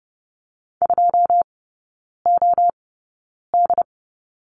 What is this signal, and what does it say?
Morse code, heard as a keyed tone, about 30 wpm, spaced as for 9 wpm, 700 Hz. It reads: 2OD
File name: 081 Lesson 14e - Number 2 - Three Characters 30wpm